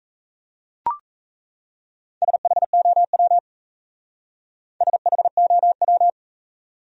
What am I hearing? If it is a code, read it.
Morse code, heard as a keyed tone, about 38 wpm, 700 Hz. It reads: SHOW SHOW